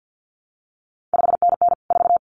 Morse code, heard as a keyed tone, about 50 wpm, 700 Hz. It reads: 5NN 4
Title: CQ WW DX Contest Exchange - 50wpm